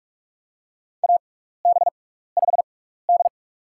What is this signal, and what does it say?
Morse code, heard as a keyed tone, about 46 wpm, spaced as for 15 wpm, 700 Hz. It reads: AB5D
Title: USA Call Signs - 4 Characters - 6x Character Spacing - 45wpm